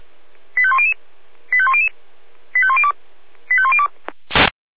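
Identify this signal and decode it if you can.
Five-tone selective call (ZVEI), heard as quick runs of five tones, tones of about 70 ms. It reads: 85100 85100 85102 85102
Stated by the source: ZVEI five-tone call 85102 received on SDR.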